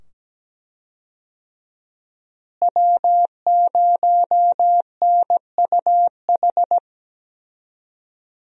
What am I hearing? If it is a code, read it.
Morse code, heard as a keyed tone, about 17 wpm, 700 Hz. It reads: W0NUH
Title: USA Call Signs - 5 Characters - Repeated 2x - 17wpm